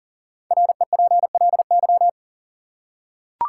Morse code, heard as a keyed tone, about 37 wpm, 700 Hz. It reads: REPLY
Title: Five Letter Words - 40wpm